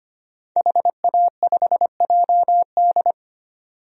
Morse code, heard as a keyed tone, about 25 wpm, 700 Hz. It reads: HA5JD
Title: DXCC-100 Call Signs - 5 Characters - 25wpm